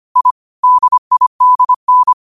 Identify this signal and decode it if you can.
Morse code, heard as a keyed tone, about 25 wpm, 1000 Hz. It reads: I DIDN
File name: 25wpm